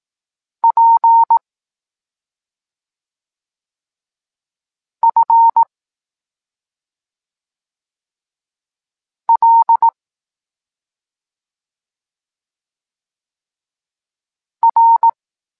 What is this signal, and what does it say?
Morse code, heard as a keyed tone, about 18 wpm, 920 Hz. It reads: P F L R
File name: Tekst-26 1